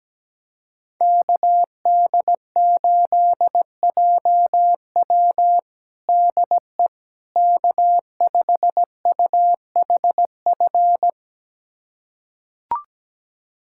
Morse code, heard as a keyed tone, about 17 wpm, 700 Hz. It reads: KD8JW DE K5UHF
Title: USA Callsign DE Callsign - 5 Characters - 17wpm